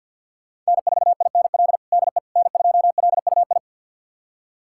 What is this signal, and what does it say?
Morse code, heard as a keyed tone, about 52 wpm, 690 Hz. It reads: N4INL DE N2LUI
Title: USA Callsign DE Callsign - 5 Characters - 50wpm